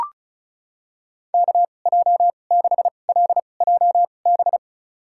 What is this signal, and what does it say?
Morse code, heard as a keyed tone, about 34 wpm, spaced as for 25 wpm, 700 Hz. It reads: KJ6LJB
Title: USA Call Signs - 6 Characters - 2x Character Spacing - 35wpm